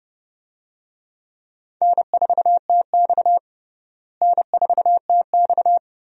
Morse code, heard as a keyed tone, about 30 wpm, 700 Hz. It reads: N4TX N4TX
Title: USA Call Signs - 4 Characters - Repeated 3x - 30wpm